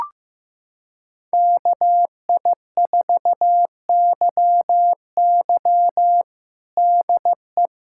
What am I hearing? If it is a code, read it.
Morse code, heard as a keyed tone, about 15 wpm, 700 Hz. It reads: KI4YY DE
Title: USA Callsign DE Callsign - 5 Characters - 15wpm